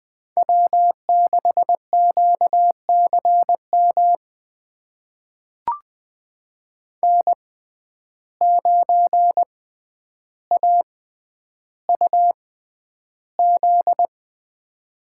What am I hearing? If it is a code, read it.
Morse code, heard as a keyed tone, about 20 wpm, spaced as for 10 wpm, 700 Hz. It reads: W6QCM N9AUZ
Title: USA Call Signs - 5 Characters - 6x Character Spacing - 20wpm